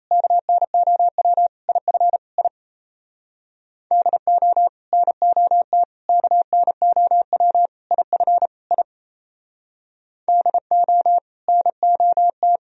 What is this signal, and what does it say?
Morse code, heard as a keyed tone, about 38 wpm, 700 Hz. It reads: KNOW IF I DO NOT KNOW IF I DO NOT